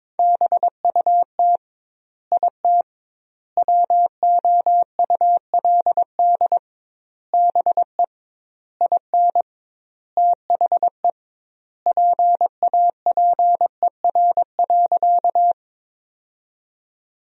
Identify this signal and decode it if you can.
Morse code, heard as a keyed tone, about 22 wpm, 700 Hz. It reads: BUT IT WOULD BE IN THE PAPER.
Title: 2x Word-Spacing - Sentences from Top 300 Words - 22wpm